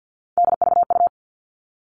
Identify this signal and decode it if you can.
Morse code, heard as a keyed tone, about 50 wpm, 700 Hz. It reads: DVU
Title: All Cut Numbers - 3-Digit 50wpm